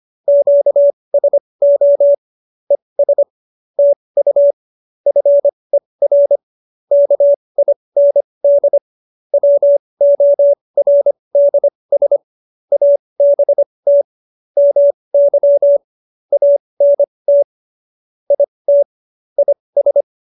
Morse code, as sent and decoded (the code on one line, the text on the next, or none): --.- ... --- / . ... / - ..- / ..-. .-. / -.- .. -. -.. / .-- --- .-. -.. ... / .- -... - / -- -.-- / .- -. - / .. - / .. ...
QSO ES TU FER KIND WORDS ABT MY ANT IT IS